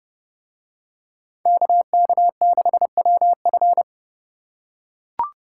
Morse code, heard as a keyed tone, about 28 wpm, 700 Hz. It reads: KK6WF